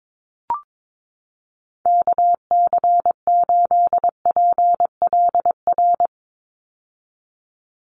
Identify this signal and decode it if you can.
Morse code, heard as a keyed tone, about 22 wpm, 700 Hz. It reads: KC8PLR
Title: USA Call Signs 22wpm